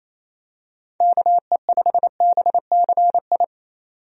Morse code, heard as a keyed tone, about 28 wpm, 700 Hz. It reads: KE5BCI
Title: USA Call Signs - 6 Characters - 28wpm